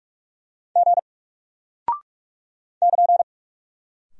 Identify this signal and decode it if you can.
Morse code, heard as a keyed tone, about 46 wpm, 700 Hz. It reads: G <KN>
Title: Single Character (Letter:Number:Punctuation:Prosign) Alt 45wpm